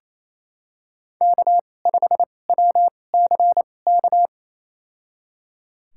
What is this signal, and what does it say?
Morse code, heard as a keyed tone, about 28 wpm, spaced as for 20 wpm, 700 Hz. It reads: K5WCK
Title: USA Call Signs - 2x Character Spacing - 28wpm